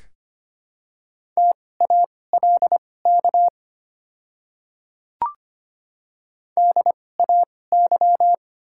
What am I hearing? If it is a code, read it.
Morse code, heard as a keyed tone, about 25 wpm, spaced as for 18 wpm, 690 Hz. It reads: TALK DAY
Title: 2x Character Spacing - Top 500 Words - 25wpm